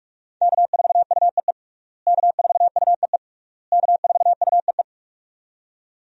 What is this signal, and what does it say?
Morse code, heard as a keyed tone, about 41 wpm, 700 Hz. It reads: K4UEE K4UEE K4UEE
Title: USA Call Signs - 5 Characters - Repeated 3x - 45wpm